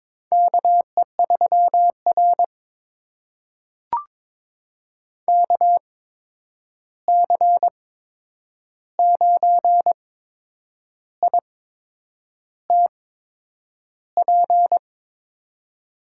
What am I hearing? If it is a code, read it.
Morse code, heard as a keyed tone, about 22 wpm, spaced as for 8 wpm, 700 Hz. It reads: KE3R KC9ITP
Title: USA Call Signs - 8x Character Spacing - 22wpm